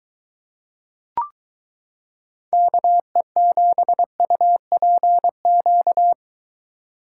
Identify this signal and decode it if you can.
Morse code, heard as a keyed tone, about 23 wpm, 700 Hz. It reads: KE7UPQ